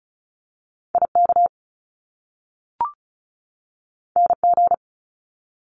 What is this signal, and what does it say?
Morse code, heard as a keyed tone, about 36 wpm, 700 Hz. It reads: IK NG
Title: Top Two Letter Combinations in Words 35wpm